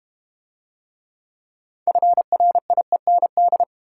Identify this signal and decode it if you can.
Morse code, heard as a keyed tone, about 32 wpm, 700 Hz. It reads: FRIEND